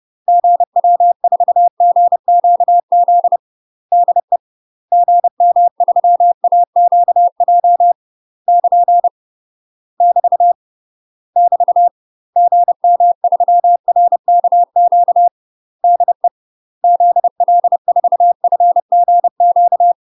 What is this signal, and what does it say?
Morse code, heard as a keyed tone, about 30 wpm, 700 Hz. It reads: GW4GQZ DE GM3AQJ <KN> <BT> <BT> GM3RKQ DE ZL4FGQ
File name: UK MASTER QSOs TRAINING / 30WPM / UK MASTER QSOs 0000